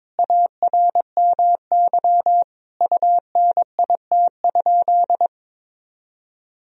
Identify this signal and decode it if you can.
Morse code, heard as a keyed tone, about 22 wpm, 700 Hz. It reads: ARMY UNIT?
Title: Sentences from Top 1000 Words - Review Part 2 - 22wpm